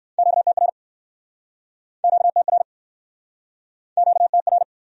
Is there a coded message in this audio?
Yes: Morse code from 0.2 to 4.6 s, heard as a keyed tone, about 64 wpm, 700 Hz. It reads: QTR QTR QTR